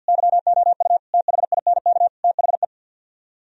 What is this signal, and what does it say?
Morse code, heard as a keyed tone, about 65 wpm, 700 Hz. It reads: YOU THINK THE